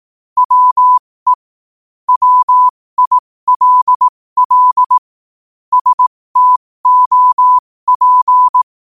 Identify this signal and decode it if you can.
Morse code, heard as a keyed tone, about 18 wpm, spaced as for 15 wpm, 1000 Hz. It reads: WE WILL STOP
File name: qotdfiles / 15wpm